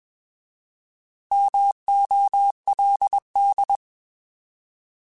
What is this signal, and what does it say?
Morse code, heard as a keyed tone, about 21 wpm, 800 Hz. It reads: MOLD
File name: Alphabetical / Alphabetical mp3s / Ms @20